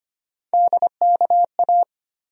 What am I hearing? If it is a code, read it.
Morse code, heard as a keyed tone, about 25 wpm, 700 Hz. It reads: DKA
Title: USA Call Sign Postfixes - 3 Letters 25wpm